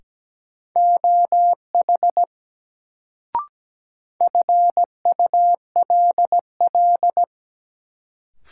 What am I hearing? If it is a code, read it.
Morse code, heard as a keyed tone, about 17 wpm, 700 Hz. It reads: OH FULL
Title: Sentences from Top 500 Words - New Word Drill - 17wpm